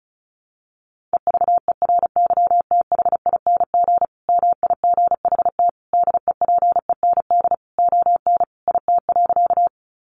Morse code, heard as a keyed tone, about 34 wpm, 700 Hz. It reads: EVERYTHING MIGHT DEPEND ON IT.
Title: Sentences from Top 1300 Words 35wpm